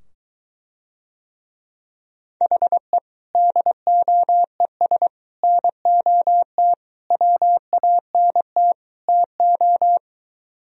Morse code, heard as a keyed tone, about 22 wpm, 700 Hz. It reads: HE DOES NOT WANT TO